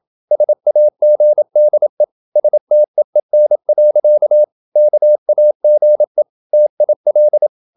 Morse code, heard as a keyed tone, about 27 wpm, 600 Hz. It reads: SAGDE STEEN. KAGE TIL
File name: Ferie QRQ 27wpm